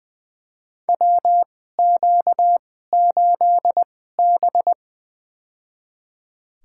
Morse code, heard as a keyed tone, about 20 wpm, spaced as for 14 wpm, 700 Hz. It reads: WQ8B
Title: USA Call Signs - 4 Characters - 2x Character Spacing - 20wpm